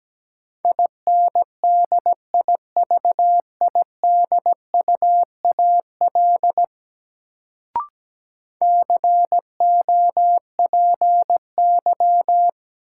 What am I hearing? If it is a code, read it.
Morse code, heard as a keyed tone, about 17 wpm, 700 Hz. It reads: INDIVIDUAL COPY
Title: Sentences from Top 1400 Words - New Word Drill Alt - 17wpm